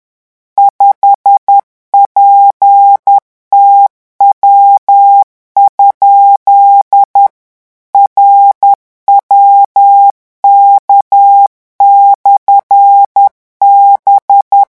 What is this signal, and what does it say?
Morse code, heard as a keyed tone, about 10 wpm, 800 Hz. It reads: 5PTW? RWK/B
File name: morse mp3 / revision13-vitesse 10 mots